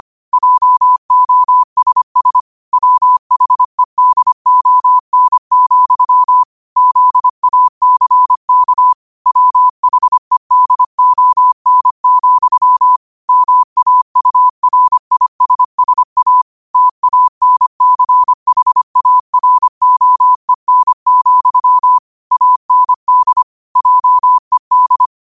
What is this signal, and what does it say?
Morse code, heard as a keyed tone, about 25 wpm, 1000 Hz. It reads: JOSS WHEDON, ZACK WHEDON, MAURISSA TANCHAROEN, AND JED